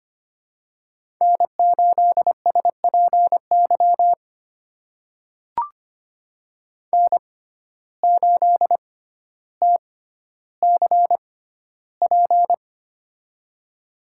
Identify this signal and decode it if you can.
Morse code, heard as a keyed tone, about 25 wpm, spaced as for 13 wpm, 700 Hz. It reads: N8SPY N8TCP
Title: USA Call Signs - 5 Characters - 6x Character Spacing - 25wpm